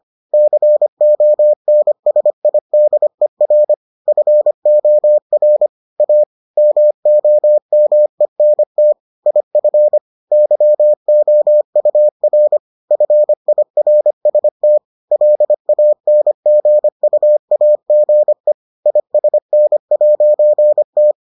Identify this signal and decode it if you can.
Morse code, heard as a keyed tone, about 25 wpm, 600 Hz. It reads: CONSIDER FOR A MOMENT IF YOUR FIRST LANGUAGE ISN'T